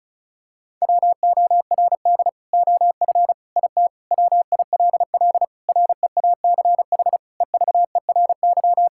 Morse code, heard as a keyed tone, about 34 wpm, 700 Hz. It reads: WORD OF IT WILL REACH EVERY
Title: Sentences from Top 1600 Words 35wpm